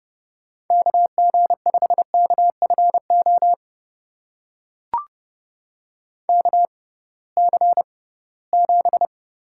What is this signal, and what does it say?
Morse code, heard as a keyed tone, about 30 wpm, spaced as for 19 wpm, 700 Hz. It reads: KG5KFO KC7
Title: USA Call Signs - 6 Characters - 6x Character Spacing - 30wpm